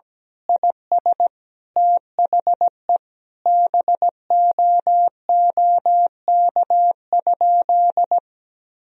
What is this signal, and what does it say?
Morse code, heard as a keyed tone, about 17 wpm, 700 Hz. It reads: IS THE BOOK?
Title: Sentences from Top 1000 Words - Review Part 1 - 17wpm